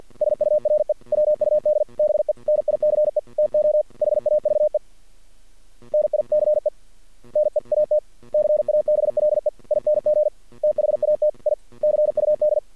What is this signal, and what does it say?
Morse code, heard as a keyed tone, about 48 wpm, 600 Hz. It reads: SURE WUD BE NICE TO HAVE NICE NEAT CABLEING THAT I CUD
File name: archived / CRSnet